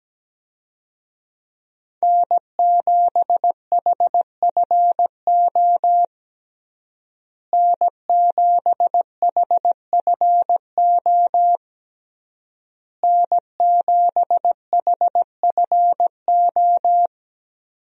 Morse code, heard as a keyed tone, about 17 wpm, 700 Hz. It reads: N7HFO N7HFO N7HFO
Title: USA Call Signs - 5 Characters - Repeated 3x - 17wpm